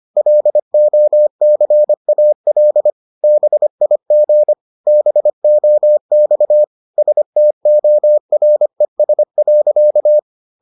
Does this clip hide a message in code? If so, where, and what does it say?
Yes: Morse code from 0.2 to 10.2 s, heard as a keyed tone, about 25 wpm, 600 Hz. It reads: LOCAL BIG BOX STORES.